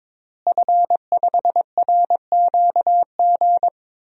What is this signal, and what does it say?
Morse code, heard as a keyed tone, about 22 wpm, 700 Hz. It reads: F5RQG
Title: DXCC-100 Call Signs - 5 Characters - Repeated 3x - 22wpm